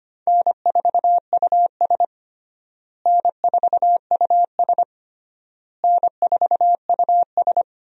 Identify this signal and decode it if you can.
Morse code, heard as a keyed tone, about 25 wpm, 700 Hz. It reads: N4US N4US N4US
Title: USA Call Signs - 4 Characters - Repeated 3x - 25wpm